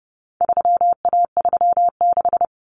Morse code, heard as a keyed tone, about 30 wpm, 700 Hz. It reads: 3A36
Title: Common Cut Numbers - 4-Digit 30wpm